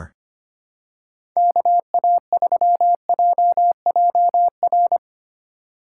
Morse code, heard as a keyed tone, about 25 wpm, 700 Hz. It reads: KA3JJR
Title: USA Call Signs - 6 Characters - 4x Character Spacing - 25wpm